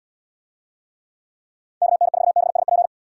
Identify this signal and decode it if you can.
Morse code, heard as a keyed tone, about 67 wpm, 700 Hz. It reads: KN4BSP